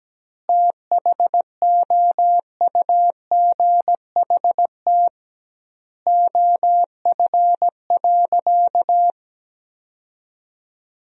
Morse code, heard as a keyed tone, about 17 wpm, 700 Hz. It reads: THOUGHT OF.
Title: 2x Word-Spacing - Sentences from Top 300 Words - 17wpm